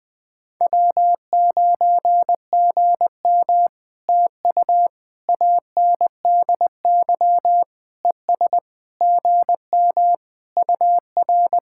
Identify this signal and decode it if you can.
Morse code, heard as a keyed tone, about 20 wpm, 700 Hz. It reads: W9GM TU ANDY ES GM UR
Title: POTA QSO Practice - 20wpm